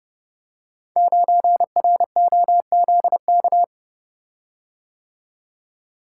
Morse code, heard as a keyed tone, about 28 wpm, 700 Hz. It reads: 9ROZK
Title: Code Groups - 5 Letter:Numbers - 30wpm